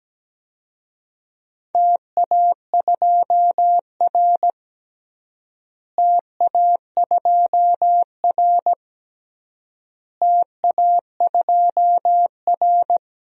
Morse code, heard as a keyed tone, about 17 wpm, 700 Hz. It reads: TA2R TA2R TA2R
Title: DXCC-100 Call Signs - 4 Characters - Repeated 3x - 17wpm